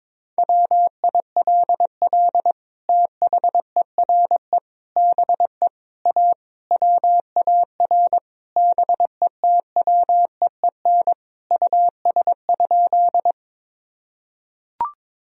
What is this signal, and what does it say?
Morse code, heard as a keyed tone, about 22 wpm, 700 Hz. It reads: WILL THERE BE A WAR BETWEEN US?
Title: Sentences from Top 1000 Words - Review Part 2 - 22wpm